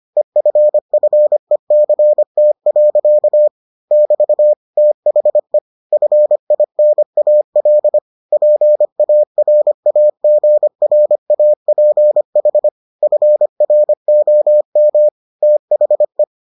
Morse code, heard as a keyed tone, about 25 wpm, 600 Hz. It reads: EFFECT. <BT> THE FINAL PARAGRAPH FROM THE